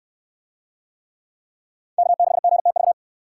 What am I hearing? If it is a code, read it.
Morse code, heard as a keyed tone, about 60 wpm, 700 Hz. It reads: K6GTV